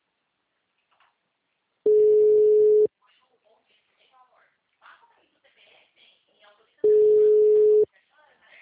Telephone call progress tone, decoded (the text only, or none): ringback tone